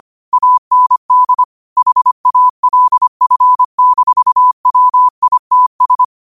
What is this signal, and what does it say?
Morse code, heard as a keyed tone, about 25 wpm, 1000 Hz. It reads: AND HALF-WITS
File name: qotdfiles / 25wpm